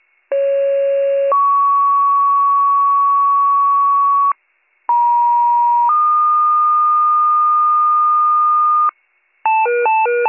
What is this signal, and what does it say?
Two-tone page, heard as a one-second tone, then a three-second tone, about 550 then 1120 Hz, about 950 then 1220 Hz. It is heard from 0.3 to 8.9 s.